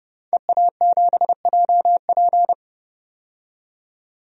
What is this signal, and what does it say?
Morse code, heard as a keyed tone, about 30 wpm, 700 Hz. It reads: EA7JP
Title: DXCC-100 Call Signs - 5 Characters - 30wpm